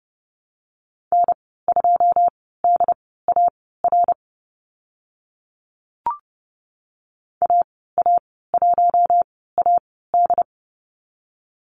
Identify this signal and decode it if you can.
Morse code, heard as a keyed tone, about 30 wpm, spaced as for 17 wpm, 690 Hz. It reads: N2DAR AA1AD
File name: Lesson 15g - Number 5 - Call Signs 30wpm